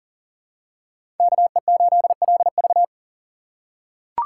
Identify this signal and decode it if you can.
Morse code, heard as a keyed tone, about 37 wpm, 700 Hz. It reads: KE8LV
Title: USA Call Signs - 3x Character Spacing - 40wpm